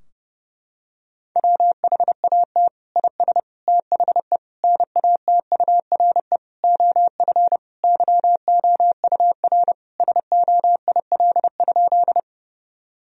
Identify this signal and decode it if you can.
Morse code, heard as a keyed tone, about 30 wpm, 700 Hz. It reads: WHAT IS THE NATURE OF YOUR SOIL?
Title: Sentences from Top 1000 Words - Review Part 4 - 30wpm